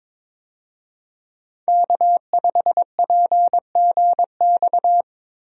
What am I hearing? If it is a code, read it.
Morse code, heard as a keyed tone, about 22 wpm, 700 Hz. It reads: K5PGX
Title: USA Call Signs - 5 Characters - Repeated 2x - 22wpm